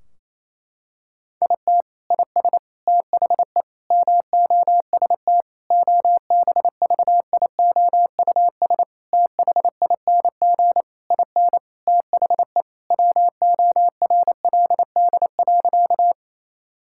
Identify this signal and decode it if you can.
Morse code, heard as a keyed tone, about 28 wpm, 700 Hz. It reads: IT IS THE MOST OBVIOUS THING IN THE WORLD.
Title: Sentences from Top 1600 Words 28wpm